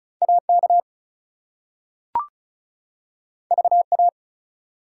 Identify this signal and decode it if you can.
Morse code, heard as a keyed tone, about 36 wpm, 700 Hz. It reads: AK VA
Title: US States and Canadian Provinces 35wpm